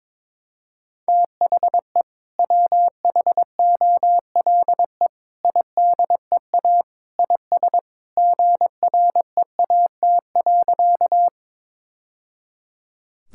Morse code, heard as a keyed tone, about 22 wpm, 700 Hz. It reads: THE WHOLE IDEA IS GREAT.
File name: practice / 22wpm / Sentences with 2-Word Phrases - 22wpm